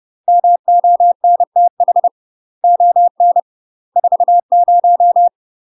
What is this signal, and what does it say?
Morse code, heard as a keyed tone, about 30 wpm, 700 Hz. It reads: MONTH ON 40
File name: UK MASTER QSOs TRAINING/30WPM